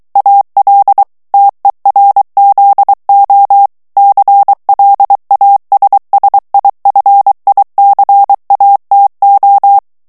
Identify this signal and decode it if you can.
Morse code, heard as a keyed tone, about 23 wpm, 800 Hz. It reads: AL TERZO CLASSIFICATO
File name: ITALIEN texte clair 2 vitesse 22 mots minute